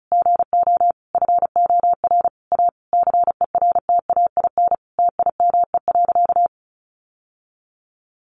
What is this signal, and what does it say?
Morse code, heard as a keyed tone, about 34 wpm, 700 Hz. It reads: GO FOR A CERTAIN TIME.
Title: Sentences from Top 1000 Words 35wpm